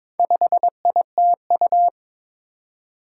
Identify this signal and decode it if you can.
Morse code, heard as a keyed tone, about 22 wpm, 700 Hz. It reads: 5ITU
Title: Four Challenging Letter-Number Combinations 22wpm